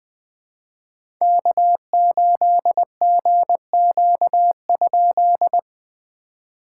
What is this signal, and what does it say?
Morse code, heard as a keyed tone, about 20 wpm, 700 Hz. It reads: K8GQ?
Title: POTA QSO Practice - 20wpm